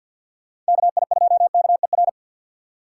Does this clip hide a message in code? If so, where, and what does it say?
Yes: Morse code from 0.7 to 2.1 s, heard as a keyed tone, about 64 wpm, 700 Hz. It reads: KI2KER